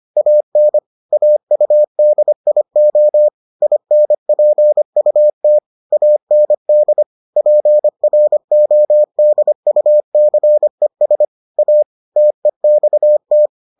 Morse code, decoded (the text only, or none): AN AUDIO INPUT AND PRODUCES A TEXT